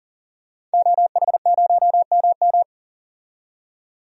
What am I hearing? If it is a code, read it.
Morse code, heard as a keyed tone, about 38 wpm, 700 Hz. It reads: OH0MM
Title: DXCC-100 Call Signs - 5 Characters - 40wpm